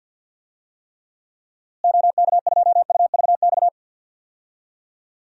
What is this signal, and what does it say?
Morse code, heard as a keyed tone, about 51 wpm, 700 Hz. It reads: OK2UVX
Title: DXCC-100 Call Signs - 6 Characters - Repeated 3x - 50wpm